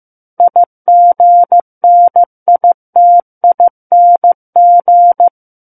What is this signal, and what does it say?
Morse code, heard as a keyed tone, about 15 wpm, 700 Hz. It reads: IGNITING
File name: Book One / 06-Chapter 6 - 15 WPM